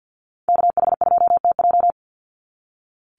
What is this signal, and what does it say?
Morse code, heard as a keyed tone, about 48 wpm, 700 Hz. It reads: KH2TJ
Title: NA SOTA Activator Callsigns 50wpm